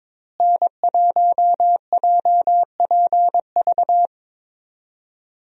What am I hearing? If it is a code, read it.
Morse code, heard as a keyed tone, about 21 wpm, 700 Hz. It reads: N1JPV
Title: USA Call Signs - 5 Characters - Repeated 3x - 22wpm